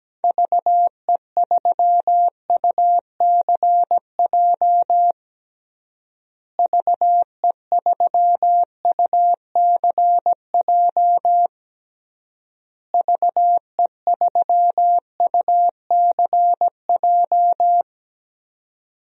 Morse code, heard as a keyed tone, about 17 wpm, 700 Hz. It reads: VE3UCJ VE3UCJ VE3UCJ
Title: DXCC-100 Call Signs - 6 Characters - Repeated 3x - 17wpm